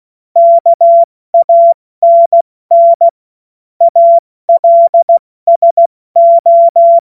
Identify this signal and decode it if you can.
Morse code, heard as a keyed tone, about 16 wpm, 680 Hz. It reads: KANN ALSO
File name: Morsekurs / Witze / Witz 21. Juni 2022 70 F80